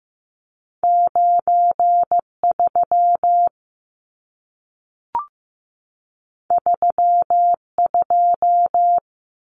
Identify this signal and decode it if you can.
Morse code, heard as a keyed tone, about 15 wpm, 700 Hz. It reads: 93 32
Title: Numbers - 2-Digit 15wpm